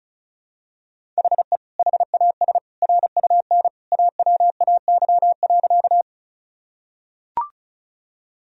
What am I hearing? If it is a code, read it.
Morse code, heard as a keyed tone, about 34 wpm, 700 Hz. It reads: HE HAS RUN AWAY.
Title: Sentences from Top 1000 Words - Review Part 1 - 35wpm